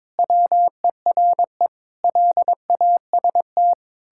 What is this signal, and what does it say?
Morse code, heard as a keyed tone, about 22 wpm, 700 Hz. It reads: WERE LAST
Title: Pangram Challenge v2 - 22wpm